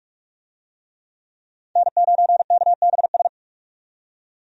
Morse code, heard as a keyed tone, about 44 wpm, 700 Hz. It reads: N9KBS
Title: USA Call Signs - 5 Characters - 45wpm